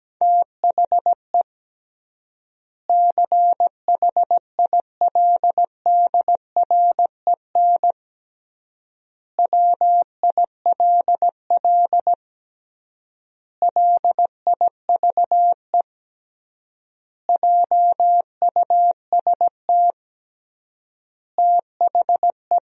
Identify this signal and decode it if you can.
Morse code, heard as a keyed tone, about 17 wpm, 700 Hz. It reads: THE CHILDREN WILL LIVE JUST THE
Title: 3x Word-Spacing - Sentences from Top 300 Words - 17wpm